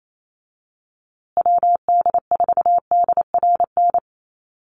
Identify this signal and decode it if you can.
Morse code, heard as a keyed tone, about 28 wpm, 700 Hz. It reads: WD4DRN
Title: USA Call Signs Repeated 2x 28wpm